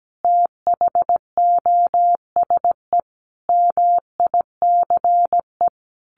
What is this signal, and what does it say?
Morse code, heard as a keyed tone, about 17 wpm, 700 Hz. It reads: THOSE MICE
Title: A Sound Of Thunder 17wpm